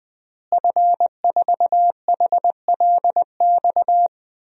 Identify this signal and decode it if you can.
Morse code, heard as a keyed tone, about 20 wpm, 700 Hz. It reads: F4HLX
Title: DXCC-100 Call Signs - 5 Characters - Repeated 2x - 20wpm